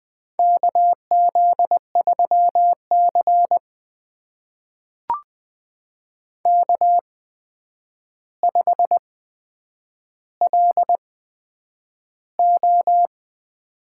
Morse code, heard as a keyed tone, about 20 wpm, spaced as for 9 wpm, 700 Hz. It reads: KZ3C K5LO
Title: USA Call Signs - 4 Characters - 8x Character Spacing - 20wpm